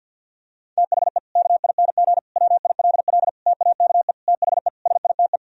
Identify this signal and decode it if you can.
Morse code, heard as a keyed tone, about 53 wpm, 700 Hz. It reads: THE KING WILL TAKE THE SITE
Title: Sentences from Top 1000 Words - Review Part 4 - 50wpm